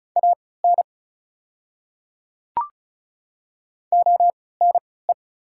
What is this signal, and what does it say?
Morse code, heard as a keyed tone, about 36 wpm, spaced as for 20 wpm, 690 Hz. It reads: AN ONE
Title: 3x Character Spacing - Top 100 Words - 35wpm